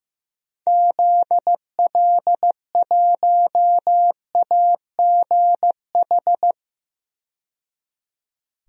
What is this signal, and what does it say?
Morse code, heard as a keyed tone, about 15 wpm, 700 Hz. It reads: ZL1AGH
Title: DXCC-100 Call Signs - 6 Characters - 15wpm